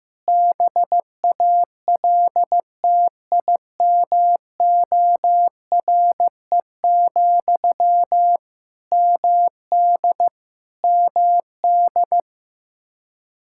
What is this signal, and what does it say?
Morse code, heard as a keyed tone, about 15 wpm, 700 Hz. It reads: BALTIMORE, MD MD
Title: Practice QSO - Exchange 1 - Beginning - 15wpm